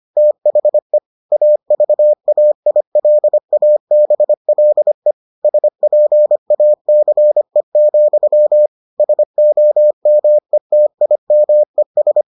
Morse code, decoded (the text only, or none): THE AVAILABLE SPACE, SOMETIMES